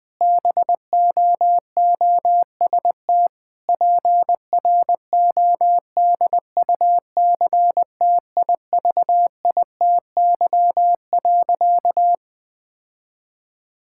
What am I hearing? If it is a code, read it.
Morse code, heard as a keyed tone, about 21 wpm, 700 Hz. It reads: BOOST PRODUCTIVITY.
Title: Difficult Sentences with Hints - 20wpm